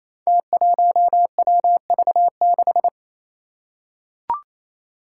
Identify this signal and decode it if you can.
Morse code, heard as a keyed tone, about 28 wpm, 700 Hz. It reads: T1WV6
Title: Five Challenging Letter-Number Combinations 28wpm